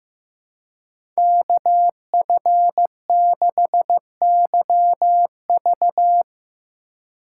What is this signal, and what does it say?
Morse code, heard as a keyed tone, about 15 wpm, 700 Hz. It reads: KF6YV
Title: USA Call Signs - 2x Character Spacing - 15wpm